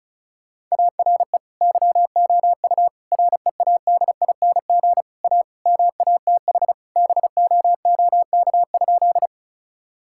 Morse code, heard as a keyed tone, about 36 wpm, 700 Hz. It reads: ARE YOU READING A MATH BOOK?
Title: Sentences from Top 1000 Words - Review Part 6 - 35wpm